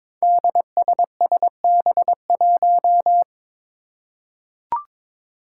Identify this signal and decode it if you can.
Morse code, heard as a keyed tone, about 21 wpm, 700 Hz. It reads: DSSB1
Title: Five Challenging Letter-Number Combinations 22wpm